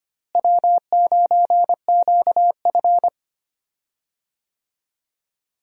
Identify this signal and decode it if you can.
Morse code, heard as a keyed tone, about 25 wpm, 700 Hz. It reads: W9QF
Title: USA Call Signs - 4 Characters - Repeated 3x - 25wpm